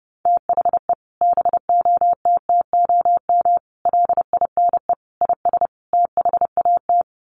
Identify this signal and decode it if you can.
Morse code, heard as a keyed tone, about 30 wpm, 700 Hz. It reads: THE BOTTOM LINE IS THAT
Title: Sets of 5 Words Encore 30wpm